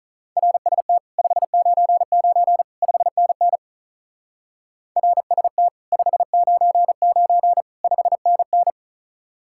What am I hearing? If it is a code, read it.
Morse code, heard as a keyed tone, about 44 wpm, 700 Hz. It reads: RST 599 5NN RST 599 5NN